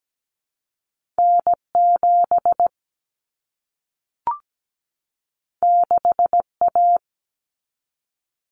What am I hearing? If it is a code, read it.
Morse code, heard as a keyed tone, about 17 wpm, 700 Hz. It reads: N7 6A
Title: Common Cut Numbers - 2-Digit 17wpm